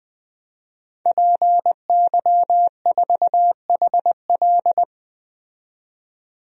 Morse code, heard as a keyed tone, about 20 wpm, 700 Hz. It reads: PY4HL
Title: DXCC-100 Call Signs - 5 Characters - 20wpm